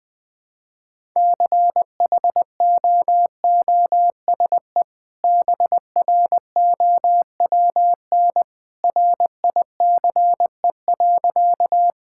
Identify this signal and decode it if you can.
Morse code, heard as a keyed tone, about 21 wpm, 700 Hz. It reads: CHOOSE BROWN RICE.
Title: Easy Sentences with Hints - 20wpm